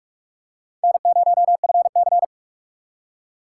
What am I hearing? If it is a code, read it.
Morse code, heard as a keyed tone, about 37 wpm, 700 Hz. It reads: N0FC
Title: Speed Racing - USA 1x2 Call Signs 30wpm